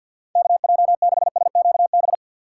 Morse code, heard as a keyed tone, about 47 wpm, 700 Hz. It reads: KJ6SQB
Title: USA Call Signs - 6 Characters - 3x Character Spacing - 50wpm